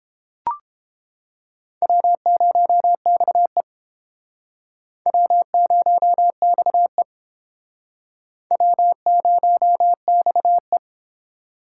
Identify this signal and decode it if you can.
Morse code, heard as a keyed tone, about 32 wpm, 700 Hz. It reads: W0XE W0XE W0XE